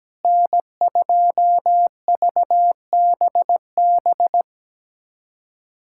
Morse code, heard as a keyed tone, about 17 wpm, 700 Hz. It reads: N2VBB
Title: USA Callsign DE Callsign - 5 Characters - 17wpm